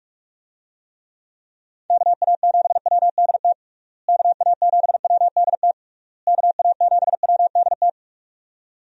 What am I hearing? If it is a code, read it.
Morse code, heard as a keyed tone, about 42 wpm, 700 Hz. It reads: KA7WDT KA7WDT KA7WDT
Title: USA Call Signs - 6 Characters - Repeated 3x - 45wpm